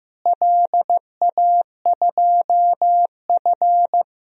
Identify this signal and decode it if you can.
Morse code, heard as a keyed tone, about 15 wpm, 700 Hz. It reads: LA2F
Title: DXCC-100 Call Signs - 4 Characters - Repeated 2x - 15wpm